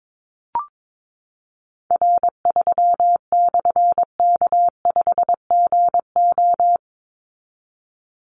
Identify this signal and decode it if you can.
Morse code, heard as a keyed tone, about 22 wpm, 700 Hz. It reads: R3/K5GO
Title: Call Signs Repeated 3x - 22wpm